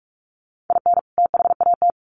Morse code, heard as a keyed tone, about 44 wpm, 700 Hz. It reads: IN THAT
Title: Sets of 2 Words 45wpm